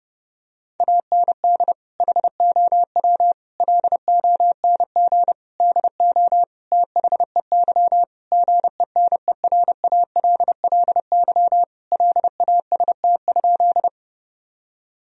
Morse code, heard as a keyed tone, about 30 wpm, 700 Hz. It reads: AND HOW LONG DO THEY GENERALLY LAST?
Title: Sentences from Top 1000 Words - Review Part 4 - 30wpm